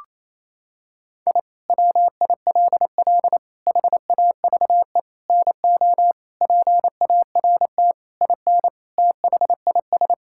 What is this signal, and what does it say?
Morse code, heard as a keyed tone, about 28 wpm, 700 Hz. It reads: I WILL HAVE NO PART IN THIS
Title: Sentences from Top 1000 Words - Review Part 5 - 28wpm